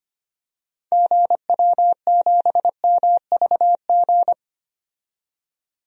Morse code, heard as a keyed tone, about 25 wpm, 700 Hz. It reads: GW7MVG
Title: DXCC-100 Call Signs - 6 Characters - 25wpm